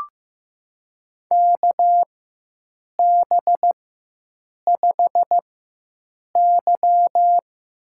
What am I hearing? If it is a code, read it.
Morse code, heard as a keyed tone, about 15 wpm, spaced as for 7 wpm, 700 Hz. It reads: KB5Y
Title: USA Call Signs - 4x Character Spacing - 15wpm